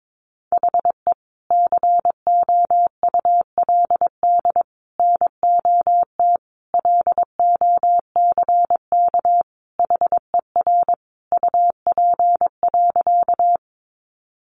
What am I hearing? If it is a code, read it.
Morse code, heard as a keyed tone, about 22 wpm, 700 Hz. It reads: HE COULD NOT LOCK HER UP.
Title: Sentences from Top 1300 Words 22wpm